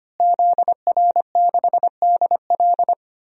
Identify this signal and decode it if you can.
Morse code, heard as a keyed tone, about 25 wpm, 700 Hz. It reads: ZR6DL
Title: DXCC-100 Call Signs - 5 Characters - Repeated 3x - 25wpm